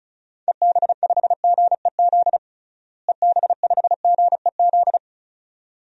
Morse code, heard as a keyed tone, about 36 wpm, 700 Hz. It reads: EB5GEZ EB5GEZ
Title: DXCC-100 Call Signs - 6 Characters - Repeated 2x - 35wpm